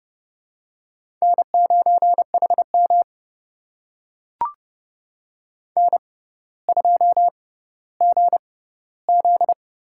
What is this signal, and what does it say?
Morse code, heard as a keyed tone, about 29 wpm, spaced as for 15 wpm, 700 Hz. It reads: N9HM N2GZ
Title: USA Call Signs - 4 Characters - 6x Character Spacing - 30wpm